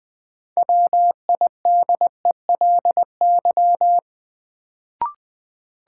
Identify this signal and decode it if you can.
Morse code, heard as a keyed tone, about 20 wpm, 700 Hz. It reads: WIDELY
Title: Sentences from Top 1200 Words - New Word Drill Alt - 20wpm